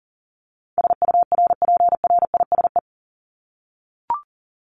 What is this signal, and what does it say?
Morse code, heard as a keyed tone, about 40 wpm, 700 Hz. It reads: SURPRISE
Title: Top 1751-2000 Words 40wpm